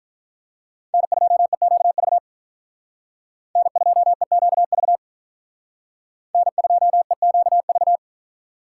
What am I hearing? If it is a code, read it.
Morse code, heard as a keyed tone, about 44 wpm, 700 Hz. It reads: N2EQV N2EQV N2EQV
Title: Speed Racing - USA 1x3 Call Signs 35wpm